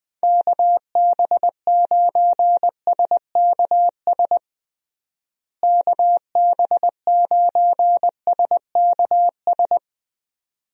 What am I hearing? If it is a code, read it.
Morse code, heard as a keyed tone, about 20 wpm, 700 Hz. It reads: KB9SKS KB9SKS
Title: USA Call Signs - 6 Characters - Repeated 2x - 20wpm